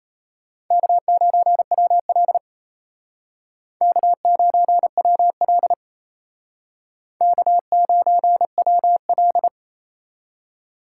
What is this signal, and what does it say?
Morse code, heard as a keyed tone, about 35 wpm, 700 Hz. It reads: K9WL K9WL K9WL
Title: Speed Racing - USA Call Signs 28wpm